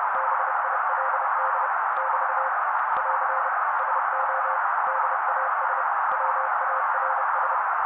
Morse code, heard as a keyed tone, about 30 wpm, 530 Hz. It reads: BILD <BT> <AR> SO DAS WARS
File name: daily / 29wpm noise